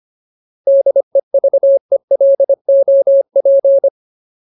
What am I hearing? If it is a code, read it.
Morse code, heard as a keyed tone, about 25 wpm, 550 Hz. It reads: DEVELOP
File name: Ultimate Word List / 25WPM Ultimate